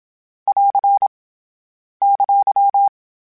Morse code, heard as a keyed tone, about 27 wpm, 800 Hz. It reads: <AR> !
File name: Difficult Combos / Difficult Combos-mp3s / AWJ ShortRun@25